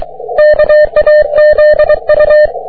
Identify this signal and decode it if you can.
Morse code, heard as a keyed tone, about 23 wpm, 600 Hz. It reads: KAZU
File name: wp / jg3qzn 30m cw